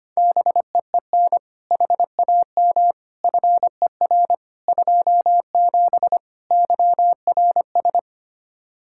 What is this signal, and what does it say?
Morse code, heard as a keyed tone, about 25 wpm, 700 Hz. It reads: BEEN HAM FER 27 YRS
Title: K1USN SST QSO Practice 25wpm